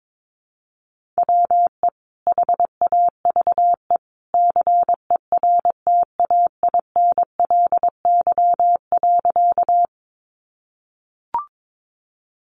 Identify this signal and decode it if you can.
Morse code, heard as a keyed tone, about 22 wpm, 700 Hz. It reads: WE HAVE CERTAINLY.
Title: Sentences from Top 900 Words 22wpm